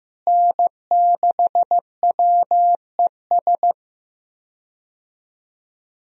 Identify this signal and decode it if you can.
Morse code, heard as a keyed tone, about 15 wpm, 700 Hz. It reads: N6WES
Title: USA Call Signs - 5 Characters - Repeated 3x - 15wpm